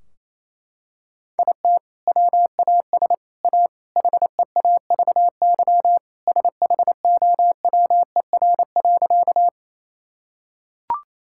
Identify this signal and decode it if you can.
Morse code, heard as a keyed tone, about 28 wpm, 700 Hz. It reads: IT WAS A HEAVY SHOWER.
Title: Sentences from Top 1600 Words 28wpm